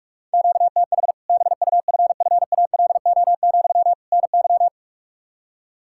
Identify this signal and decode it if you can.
Morse code, heard as a keyed tone, about 46 wpm, 700 Hz. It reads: QTH BUFFALO, NY
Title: Practice QSO - Exchange 1 - Beginning - 45wpm